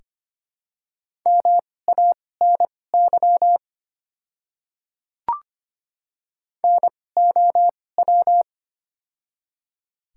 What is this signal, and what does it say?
Morse code, heard as a keyed tone, about 25 wpm, spaced as for 18 wpm, 700 Hz. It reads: MANY NOW
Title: 2x Character Spacing - Top 100 Words - 25wpm